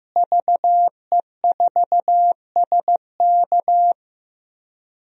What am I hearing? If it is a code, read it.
Morse code, heard as a keyed tone, about 15 wpm, 700 Hz. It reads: VE4SK
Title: DXCC-100 Call Signs - 5 Characters - Repeated 2x - 15wpm